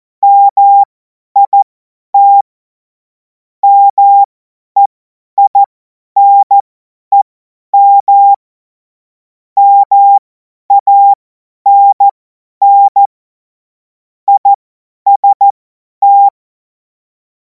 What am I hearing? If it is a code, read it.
Morse code, heard as a keyed tone, about 14 wpm, spaced as for 10 wpm, 800 Hz. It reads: MIT MEINEM MANN IST
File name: 50 F70